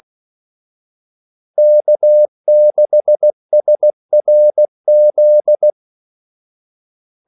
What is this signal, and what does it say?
Morse code, heard as a keyed tone, about 16 wpm, 600 Hz. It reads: K6SRZ
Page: Kaldesignaler 16 wpm